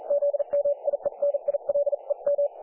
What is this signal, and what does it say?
Morse code, heard as a keyed tone, about 39 wpm, spaced as for 35 wpm, 570 Hz. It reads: ZMIENILEM
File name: dlazaawansowanych / nagraniazpasma / websdr recording 3564.7kHz